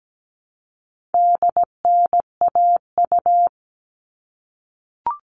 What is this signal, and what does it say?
Morse code, heard as a keyed tone, about 17 wpm, 700 Hz. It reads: DNAU
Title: All Cut Numbers - 4-Digit 17wpm